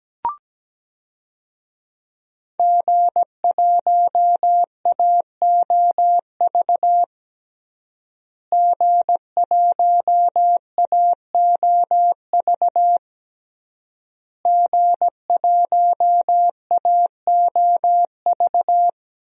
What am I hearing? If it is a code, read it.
Morse code, heard as a keyed tone, about 17 wpm, 700 Hz. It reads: G1AOV G1AOV G1AOV
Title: DXCC-100 Call Signs - 5 Characters - Repeated 3x - 17wpm